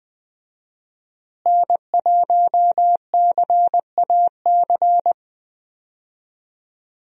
Morse code, heard as a keyed tone, about 20 wpm, 700 Hz. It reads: N1CAC
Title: USA Call Signs - 5 Characters - Repeated 2x - 20wpm